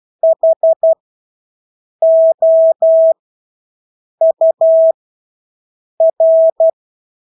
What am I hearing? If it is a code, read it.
Morse code, heard as a keyed tone, about 12 wpm, spaced as for 6 wpm, 650 Hz. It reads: HOUR
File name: Engelske ord 12-6 wpm